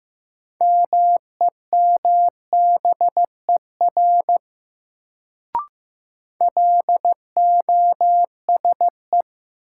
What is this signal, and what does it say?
Morse code, heard as a keyed tone, about 15 wpm, 700 Hz. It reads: MEMBER LOSE
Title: Sentences from Top 1100 Words - New Word Drill - 15wpm